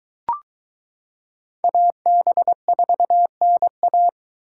Morse code, heard as a keyed tone, about 22 wpm, 700 Hz. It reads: AB4NA